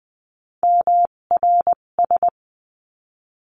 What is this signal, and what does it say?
Morse code, heard as a keyed tone, about 20 wpm, spaced as for 17 wpm, 700 Hz. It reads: MRS
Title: 257 Lesson 42b - Words 14-20wpm